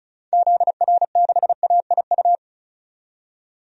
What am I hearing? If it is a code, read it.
Morse code, heard as a keyed tone, about 36 wpm, 700 Hz. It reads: ZR6AIU
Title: DXCC-100 Call Signs - 6 Characters - 35wpm